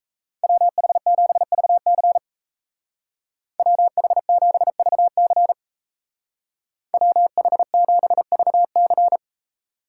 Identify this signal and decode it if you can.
Morse code, heard as a keyed tone, about 37 wpm, 700 Hz. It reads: WH7VC WH7VC WH7VC